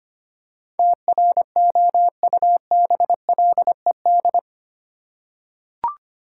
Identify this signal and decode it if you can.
Morse code, heard as a keyed tone, about 25 wpm, 700 Hz. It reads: TROUBLED
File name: Eight Letter Words - 25wpm